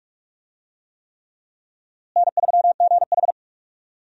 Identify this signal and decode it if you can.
Morse code, heard as a keyed tone, about 46 wpm, 700 Hz. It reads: N3GH
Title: USA Call Signs - 4 Characters - Repeated 3x - 45wpm